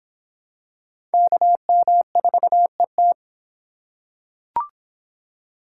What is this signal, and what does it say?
Morse code, heard as a keyed tone, about 26 wpm, 700 Hz. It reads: KM4ET